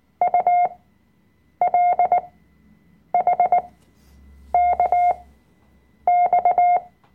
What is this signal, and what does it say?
Morse code, heard as a keyed tone, about 19 wpm, spaced as for 7 wpm, 710 Hz. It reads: ULHKX
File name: Workshop Dateien / CW-Online-Kurs / Lektion 005